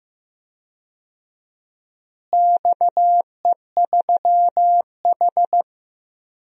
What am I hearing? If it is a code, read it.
Morse code, heard as a keyed tone, about 15 wpm, 700 Hz. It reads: XE3H
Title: DXCC-100 Call Signs - 4 Characters - Repeated 2x - 15wpm